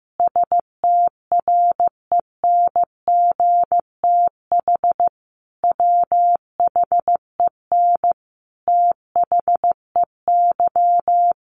Morse code, heard as a keyed tone, about 15 wpm, 700 Hz. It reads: STRENGTH WHEN THEY
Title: Sentences from Top 1300 Words 15wpm